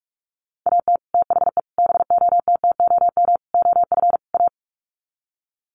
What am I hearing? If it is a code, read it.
Morse code, heard as a keyed tone, about 44 wpm, 690 Hz. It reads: AT THE BOTTOM OF A
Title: Sets of 5 Words Encore2 45wpm